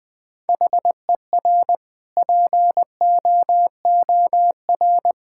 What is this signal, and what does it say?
Morse code, heard as a keyed tone, about 20 wpm, 700 Hz. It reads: HER POOR
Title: Sentences from Top 1000 Words - Review Part 6 - 20wpm